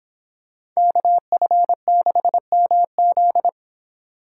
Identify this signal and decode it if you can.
Morse code, heard as a keyed tone, about 26 wpm, 700 Hz. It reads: KF6MZ